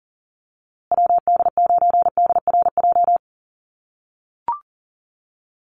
Morse code, heard as a keyed tone, about 40 wpm, 700 Hz. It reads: WD9DRJ
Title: USA Call Signs Repeated 3x 40wpm